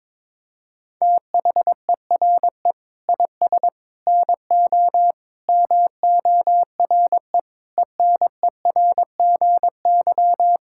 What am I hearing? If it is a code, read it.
Morse code, heard as a keyed tone, about 22 wpm, 700 Hz. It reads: THERE IS NO MORE ENERGY
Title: Sentences from Top 1000 Words - Review Part 6 - 22wpm